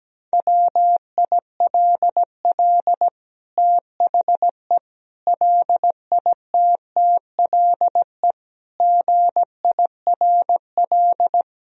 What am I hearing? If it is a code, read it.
Morse code, heard as a keyed tone, about 17 wpm, 700 Hz. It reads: WILL THE LITTLE GIRL
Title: Sentences from Top 1000 Words - Review Part 1 - 17wpm